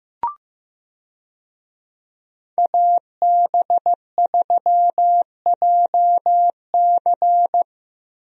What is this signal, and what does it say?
Morse code, heard as a keyed tone, about 15 wpm, 700 Hz. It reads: AB3JC
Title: USA Call Signs - 5 Characters - Repeated 2x - 15wpm